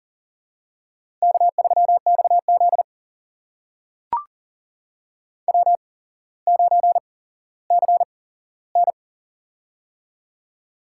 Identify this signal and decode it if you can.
Morse code, heard as a keyed tone, about 40 wpm, spaced as for 17 wpm, 700 Hz. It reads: K3XZ W9CN
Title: USA Call Signs - 4 Characters - 8x Character Spacing - 40wpm